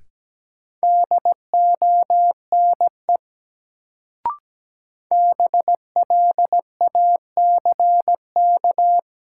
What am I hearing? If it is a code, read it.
Morse code, heard as a keyed tone, about 17 wpm, 700 Hz. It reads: DONE BLACK
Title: Sets of 5 Words Encore - New Word Drill - 17wpm